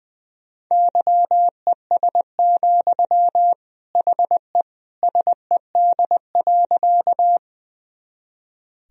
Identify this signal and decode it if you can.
Morse code, heard as a keyed tone, about 20 wpm, 700 Hz. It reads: YES, HE SED.
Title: Sentences from Top 500 Words and Abbreviations 20wpm